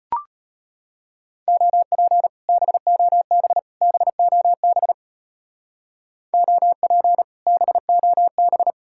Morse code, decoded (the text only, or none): OP BOB BOB OP BOB